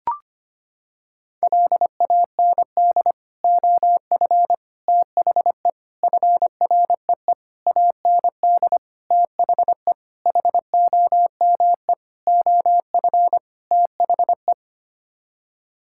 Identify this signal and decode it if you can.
Morse code, heard as a keyed tone, about 25 wpm, 700 Hz. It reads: LAND OF THE FREE AND THE HOME OF THE
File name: Sets of 9 Words - 25wpm